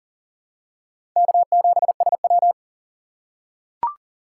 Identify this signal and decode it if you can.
Morse code, heard as a keyed tone, about 40 wpm, 700 Hz. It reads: K7SW
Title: USA Call Signs - 4 Characters - 4x Character Spacing - 40wpm